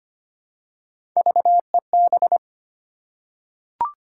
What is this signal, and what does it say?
Morse code, heard as a keyed tone, about 25 wpm, 700 Hz. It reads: VEB
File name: Three Letter-Number Combinations - 25wpm